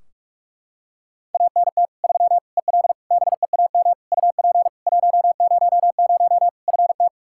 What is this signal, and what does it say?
Morse code, heard as a keyed tone, about 41 wpm, 700 Hz. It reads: ANT 3 EL BEAM UP 100 FT
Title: K1USN SST QSO Practice 45wpm